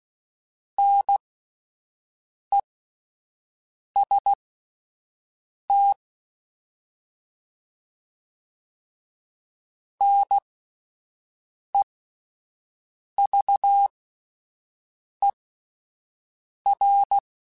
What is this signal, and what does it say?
Morse code, heard as a keyed tone, about 16 wpm, spaced as for 5 wpm, 800 Hz. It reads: NEST NEVER